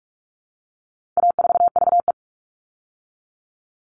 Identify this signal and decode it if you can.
Morse code, heard as a keyed tone, about 44 wpm, 700 Hz. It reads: A4VE
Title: All Cut Numbers - 4-Digit 45wpm